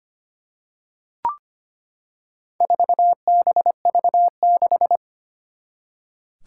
Morse code, heard as a keyed tone, about 25 wpm, 700 Hz. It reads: 4BV6
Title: Four Challenging Letter-Number Combinations 25wpm